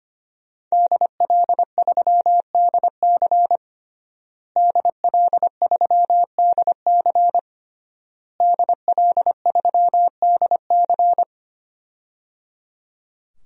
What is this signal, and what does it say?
Morse code, heard as a keyed tone, about 25 wpm, 700 Hz. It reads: DL3DC DL3DC DL3DC
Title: DXCC-100 Call Signs - 5 Characters - Repeated 3x - 25wpm